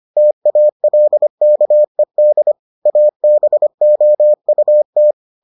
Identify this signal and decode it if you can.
Morse code, heard as a keyed tone, about 25 wpm, 600 Hz. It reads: TALKED ABOUT